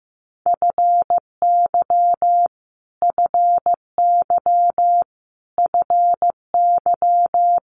Morse code, heard as a keyed tone, about 15 wpm, 700 Hz. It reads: FY FY FY
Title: Multi-letter Phonemes Repeated 3x 15wpm